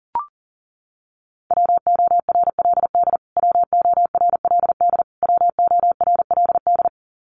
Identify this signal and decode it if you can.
Morse code, heard as a keyed tone, about 40 wpm, 700 Hz. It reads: WORLD WORLD WORLD
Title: Top 200 Words Repeated 40wpm